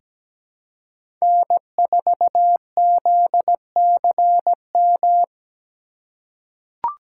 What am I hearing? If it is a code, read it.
Morse code, heard as a keyed tone, about 17 wpm, 700 Hz. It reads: N4ZCM
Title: USA Call Signs - 5 Characters - 4x Character Spacing - 17wpm